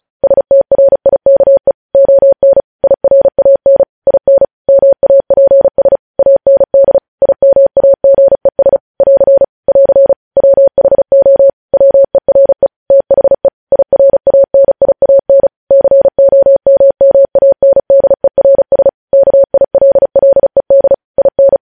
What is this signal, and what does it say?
Morse code, heard as a keyed tone, about 36 wpm, 550 Hz. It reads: STRIKE ON IRAN IN MAPS AND IMAGES <AR> <AR> WHO WERE THE IRANIAN COMMANDERS KILLED IN
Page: news headlines in morse code at 35 wpm